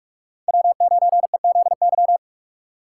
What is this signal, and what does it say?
Morse code, heard as a keyed tone, about 42 wpm, 700 Hz. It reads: W9EZY
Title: USA Call Signs - 5 Characters - 6x Character Spacing - 45wpm